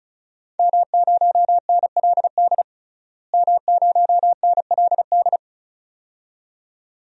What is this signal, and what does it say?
Morse code, heard as a keyed tone, about 36 wpm, 700 Hz. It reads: M0NLD M0NLD
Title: DXCC-100 Call Signs - 5 Characters - Repeated 2x - 35wpm